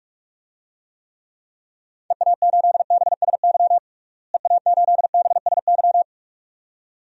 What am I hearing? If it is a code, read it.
Morse code, heard as a keyed tone, about 42 wpm, 700 Hz. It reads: EA8BSY EA8BSY
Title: DXCC-100 Call Signs - 6 Characters - Repeated 2x - 45wpm